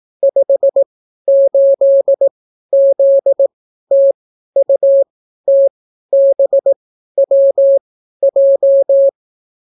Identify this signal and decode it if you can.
Morse code, heard as a keyed tone, about 18 wpm, spaced as for 12 wpm, 550 Hz. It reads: 58ZTUTBWJ